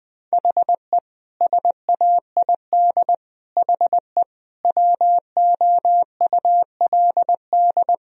Morse code, heard as a keyed tone, about 20 wpm, 700 Hz. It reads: HE SAID HE WOULD
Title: Sentences from Top 1000 Words - Review Part 4 - 20wpm